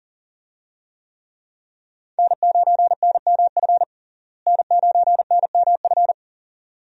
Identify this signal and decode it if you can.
Morse code, heard as a keyed tone, about 37 wpm, 700 Hz. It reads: N9NMF N9NMF
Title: USA Call Signs - 5 Characters - Repeated 2x - 40wpm